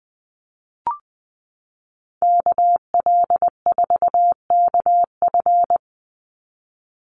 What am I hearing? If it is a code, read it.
Morse code, heard as a keyed tone, about 20 wpm, 700 Hz. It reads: KL4KF
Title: USA Call Signs 20wpm